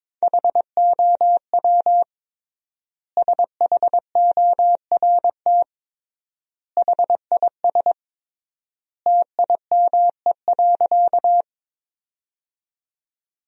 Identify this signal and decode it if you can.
Morse code, heard as a keyed tone, about 21 wpm, 700 Hz. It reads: HOW SHORT HIS TIME.
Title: 3x Word-Spacing - Sentences from Top 400 Words - 22wpm